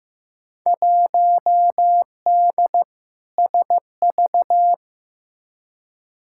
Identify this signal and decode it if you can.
Morse code, heard as a keyed tone, about 15 wpm, 700 Hz. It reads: 1D SV
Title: Field Day Exchanges - 15wpm